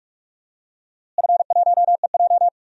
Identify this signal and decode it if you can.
Morse code, heard as a keyed tone, about 44 wpm, 700 Hz. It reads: F1EJ
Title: DXCC-100 Call Signs - 4 Characters - Repeated 2x - 45wpm